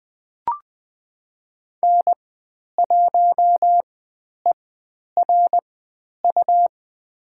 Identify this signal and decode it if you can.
Morse code, heard as a keyed tone, about 20 wpm, spaced as for 10 wpm, 700 Hz. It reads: N1ERU
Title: 109 Lesson 18g - Letter m - Call Signs 20wpm